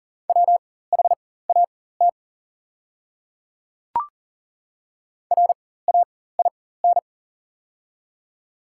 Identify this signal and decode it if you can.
Morse code, heard as a keyed tone, about 40 wpm, spaced as for 19 wpm, 690 Hz. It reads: WHAT RAIN
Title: 4x Character Spacing - Top 300 Words - 40wpm